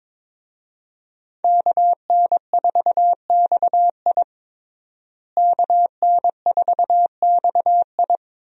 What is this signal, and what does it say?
Morse code, heard as a keyed tone, about 21 wpm, 700 Hz. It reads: KN4XI KN4XI
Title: USA Call Signs - 5 Characters - Repeated 3x - 22wpm